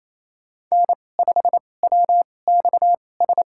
Morse code, heard as a keyed tone, about 28 wpm, spaced as for 20 wpm, 700 Hz. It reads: N5WXS
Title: USA Call Signs - 5 Characters - 2x Character Spacing - 28wpm